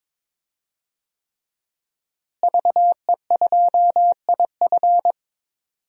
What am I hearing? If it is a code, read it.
Morse code, heard as a keyed tone, about 22 wpm, 700 Hz. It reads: VE2IF
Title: DXCC-100 Call Signs - 5 Characters - Repeated 3x - 22wpm